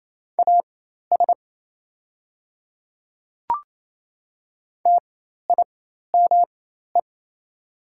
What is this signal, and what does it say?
Morse code, heard as a keyed tone, about 28 wpm, spaced as for 13 wpm, 700 Hz. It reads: AS TIME
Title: 4x Character Spacing - Top 100 Words - 28wpm